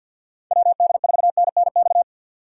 Morse code, heard as a keyed tone, about 47 wpm, 700 Hz. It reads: WD4NNX
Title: USA Call Signs - 6 Characters - 3x Character Spacing - 50wpm